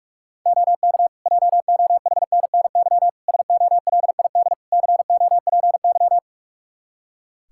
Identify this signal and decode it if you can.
Morse code, heard as a keyed tone, about 47 wpm, 700 Hz. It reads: OK JOHNNY SOLID COPY
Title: Practice QSO - Exchange 3 - More Info - 45wpm